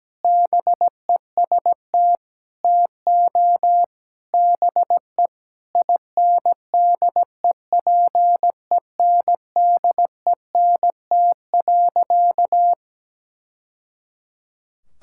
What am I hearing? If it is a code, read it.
Morse code, heard as a keyed tone, about 17 wpm, 700 Hz. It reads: BEST TO BE INDEPENDENT.
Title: Sentences from Top 1000 Words - Review Part 6 - 17wpm